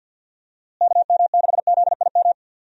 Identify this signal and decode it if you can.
Morse code, heard as a keyed tone, about 47 wpm, 700 Hz. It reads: KM6ZIM
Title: USA Call Signs - 6 Characters - Repeated 2x - 50wpm